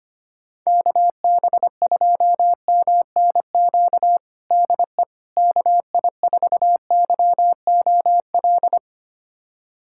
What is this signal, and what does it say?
Morse code, heard as a keyed tone, about 25 wpm, 700 Hz. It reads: KB2MNQ DE KI4YOL
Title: USA Callsign DE Callsign - 6 Characters - 25wpm